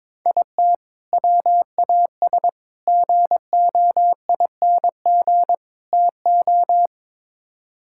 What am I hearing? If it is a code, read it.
Morse code, heard as a keyed tone, about 22 wpm, 700 Hz. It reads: IT WAS GOING TO
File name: practice / 22wpm / Build A Phrase - 5-Word Phrases - 22wpm